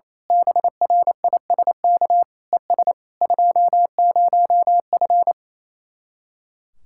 Morse code, heard as a keyed tone, about 26 wpm, 700 Hz. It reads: BRISK ES 20F
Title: Speed-Racing Common QSO Phrases Encore - 28wpm